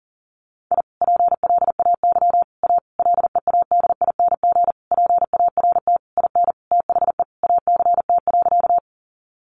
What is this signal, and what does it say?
Morse code, heard as a keyed tone, about 40 wpm, 700 Hz. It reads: I PLAY A LEADING PART IN THE ACT.
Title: Sentences from Top 1100 Words 40wpm